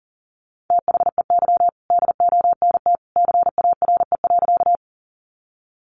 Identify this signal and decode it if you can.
Morse code, heard as a keyed tone, about 40 wpm, 700 Hz. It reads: THEY DONT CARE.
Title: Sentences from Top 300 Words 40wpm